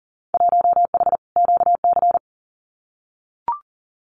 Morse code, heard as a keyed tone, about 37 wpm, 700 Hz. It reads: 1H QC
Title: Winter Field Day Exchanges 40wpm